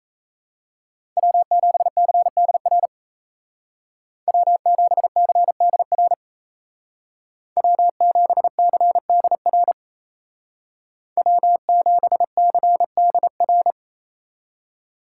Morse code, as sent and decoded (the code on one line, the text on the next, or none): .-- --... -.-. -.. .-. / .-- --... -.-. -.. .-. / .-- --... -.-. -.. .-. / .-- --... -.-. -.. .-.
W7CDR W7CDR W7CDR W7CDR